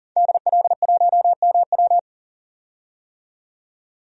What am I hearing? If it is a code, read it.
Morse code, heard as a keyed tone, about 40 wpm, 700 Hz. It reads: DL1MW
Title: DXCC-100 Call Signs - 5 Characters - 40wpm